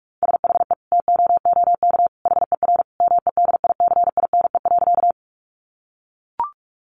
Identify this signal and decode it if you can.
Morse code, heard as a keyed tone, about 44 wpm, 700 Hz. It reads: SHE TOOK HER MEDICINE.
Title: Sentences from Top 900 Words 45wpm